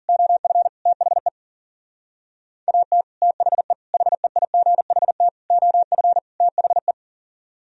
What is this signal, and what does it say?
Morse code, heard as a keyed tone, about 49 wpm, 700 Hz. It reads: OF THE AT THE HEIGHT OF THE
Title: Speed-Racing - Sets of 5 Words 40wpm